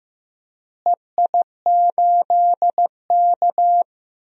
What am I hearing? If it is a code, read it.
Morse code, heard as a keyed tone, about 15 wpm, 700 Hz. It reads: EI8K
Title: DXCC-100 Call Signs - 4 Characters - Repeated 3x - 15wpm